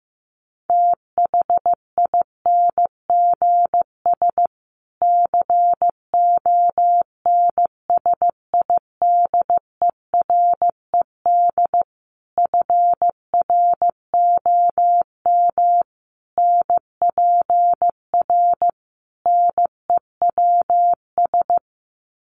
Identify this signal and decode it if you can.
Morse code, heard as a keyed tone, about 15 wpm, 700 Hz. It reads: THINGS CONSIDERED FROM NPR NEWS
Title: Sets of 5 Words Encore 15wpm